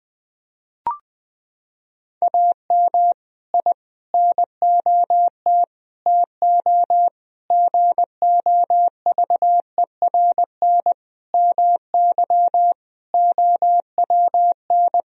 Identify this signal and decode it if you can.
Morse code, heard as a keyed tone, about 20 wpm, 700 Hz. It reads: AM I NOT TO GOVERN MY OWN
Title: Sentences from Top 1000 Words - Review Part 2 - 20wpm